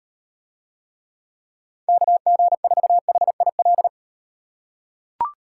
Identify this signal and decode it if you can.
Morse code, heard as a keyed tone, about 37 wpm, 700 Hz. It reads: KG4HIL